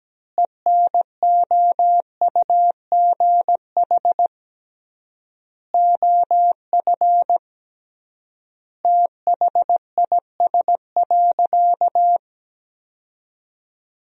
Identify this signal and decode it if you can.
Morse code, heard as a keyed tone, about 17 wpm, 700 Hz. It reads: ENOUGH OF THIS.
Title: 3x Word-Spacing - Sentences from Top 400 Words - 17wpm